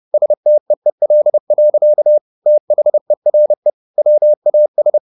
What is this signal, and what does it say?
Morse code, heard as a keyed tone, about 30 wpm, 600 Hz. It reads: STEEL. THERE WAS
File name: the old man and the sea / the old man and the sea 1 30wpm